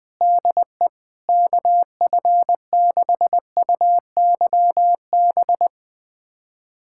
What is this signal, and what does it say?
Morse code, heard as a keyed tone, about 20 wpm, 700 Hz. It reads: DE KF6UYB
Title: USA Callsign DE Callsign - 6 Characters - 20wpm